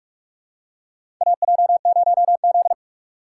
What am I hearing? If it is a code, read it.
Morse code, heard as a keyed tone, about 42 wpm, 700 Hz. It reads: AJ0Z
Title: USA Call Signs - 4 Characters - 3x Character Spacing - 45wpm